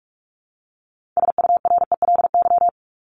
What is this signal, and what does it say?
Morse code, heard as a keyed tone, about 44 wpm, 700 Hz. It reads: SURELY
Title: LY Suffix - Top 100 Words with Base Words 45wpm